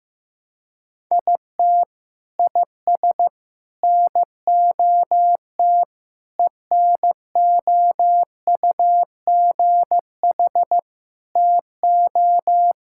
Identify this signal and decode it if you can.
Morse code, heard as a keyed tone, about 15 wpm, 700 Hz. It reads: IT IS NOT ENOUGH TO
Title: Sentences from Top 1000 Words - Review Part 6 - 15wpm